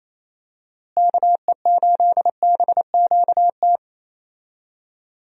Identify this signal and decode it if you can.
Morse code, heard as a keyed tone, about 28 wpm, 700 Hz. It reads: KE8BQT
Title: USA Call Signs - 6 Characters - Repeated 2x - 28wpm